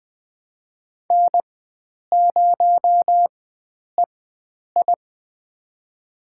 Morse code, heard as a keyed tone, about 21 wpm, spaced as for 9 wpm, 700 Hz. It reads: N0EI
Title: USA Call Signs - 4x Character Spacing - 20wpm